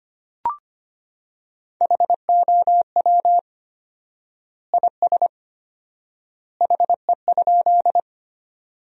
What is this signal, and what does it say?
Morse code, heard as a keyed tone, about 25 wpm, 700 Hz. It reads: HOW IS HE?
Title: 4x Word-Spacing - Sentences from Top 100 Words - 25wpm